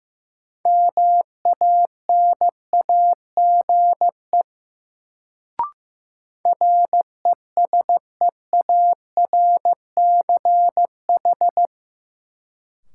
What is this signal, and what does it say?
Morse code, heard as a keyed tone, about 15 wpm, 700 Hz. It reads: MANAGE RESEARCH
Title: Sentences from Top 1400 Words - New Word Drill Alt - 15wpm